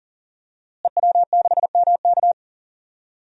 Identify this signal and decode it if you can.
Morse code, heard as a keyed tone, about 40 wpm, 700 Hz. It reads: EW6MK
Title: DXCC-100 Call Signs - 5 Characters - Repeated 3x - 40wpm